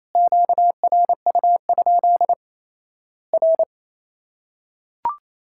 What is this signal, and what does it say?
Morse code, heard as a keyed tone, about 27 wpm, 700 Hz. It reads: QRU? R
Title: Worked All Europe DX Contest - QTC Exchange - 28wpm